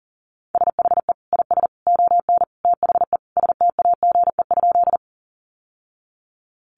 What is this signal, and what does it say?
Morse code, heard as a keyed tone, about 40 wpm, 700 Hz. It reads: SHE IS ON THE STAGE?
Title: Sentences from Top 1200 Words 40wpm